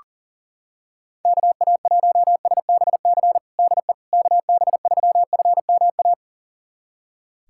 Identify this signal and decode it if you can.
Morse code, heard as a keyed tone, about 40 wpm, 700 Hz. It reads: KA1SBC DE KB3FMA
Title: USA Callsign DE Callsign - 6 Characters - 40wpm